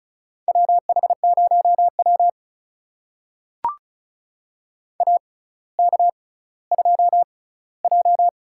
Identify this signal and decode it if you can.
Morse code, heard as a keyed tone, about 36 wpm, spaced as for 18 wpm, 700 Hz. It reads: WH0W AK2J
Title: USA Call Signs - 4 Characters - 6x Character Spacing - 35wpm